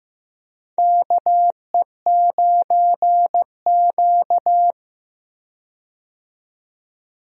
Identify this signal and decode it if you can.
Morse code, heard as a keyed tone, about 16 wpm, 700 Hz. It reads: KE9Q
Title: USA Call Signs - 4 Characters - 15wpm